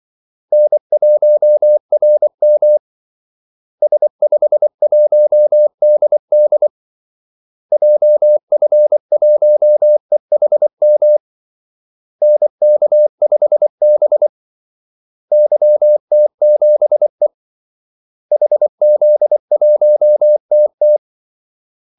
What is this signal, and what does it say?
Morse code, heard as a keyed tone, about 24 wpm, 600 Hz. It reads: N1RM S51DD JF1EHM NK5B YT7E HZ1TT
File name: Callsigns-24wpm